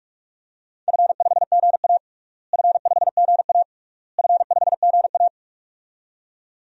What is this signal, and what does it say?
Morse code, heard as a keyed tone, about 44 wpm, 700 Hz. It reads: F5GA F5GA F5GA
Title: DXCC-100 Call Signs - 4 Characters - Repeated 3x - 45wpm